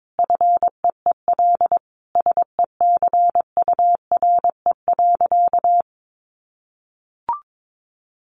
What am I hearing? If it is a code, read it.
Morse code, heard as a keyed tone, about 22 wpm, 700 Hz. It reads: FEEL SECURE.
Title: Sentences from Top 1500 Words 22wpm